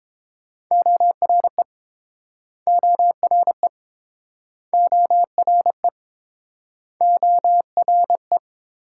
Morse code, heard as a keyed tone, about 30 wpm, 700 Hz. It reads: ORE ORE ORE ORE